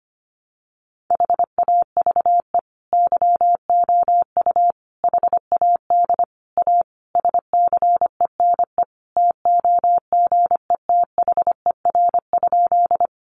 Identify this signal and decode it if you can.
Morse code, heard as a keyed tone, about 25 wpm, 700 Hz. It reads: HAVE YOU HAD A SCENE TOGETHER?
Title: Sentences from Top 1100 Words 25wpm